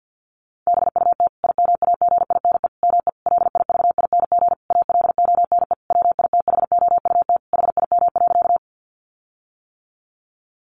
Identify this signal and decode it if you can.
Morse code, heard as a keyed tone, about 50 wpm, 700 Hz. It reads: BUT IMAGINE ME LIVING ALONE WITHOUT HIM.
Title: Sentences from Top 1400 Words 50wpm